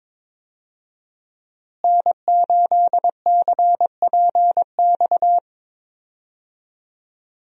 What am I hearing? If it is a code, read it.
Morse code, heard as a keyed tone, about 22 wpm, 700 Hz. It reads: N8CPX
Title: USA Call Signs - 5 Characters - Repeated 3x - 22wpm